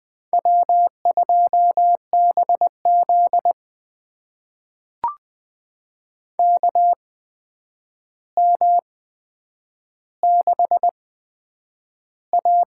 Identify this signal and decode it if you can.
Morse code, heard as a keyed tone, about 20 wpm, spaced as for 9 wpm, 700 Hz. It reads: W2BZ KM6A
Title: USA Call Signs - 4 Characters - 8x Character Spacing - 20wpm